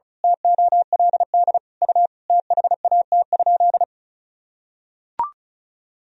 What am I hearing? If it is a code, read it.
Morse code, heard as a keyed tone, about 36 wpm, 700 Hz. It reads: TOLD U THAT?
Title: Sentences from Top 400 Words and Abbreviations 35wpm